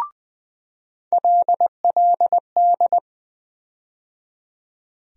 Morse code, heard as a keyed tone, about 20 wpm, 700 Hz. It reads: LLD
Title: USA Call Sign Postfixes - 3 Letters 20wpm